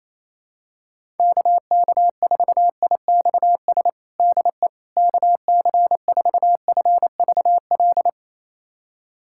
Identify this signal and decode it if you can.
Morse code, heard as a keyed tone, about 28 wpm, 700 Hz. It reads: KK4IXS DE KC4FVL
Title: USA Callsign DE Callsign - 6 Characters - 28wpm